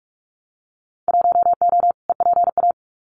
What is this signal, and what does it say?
Morse code, heard as a keyed tone, about 46 wpm, 700 Hz. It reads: 1O EPA
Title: Winter Field Day Exchanges 45wpm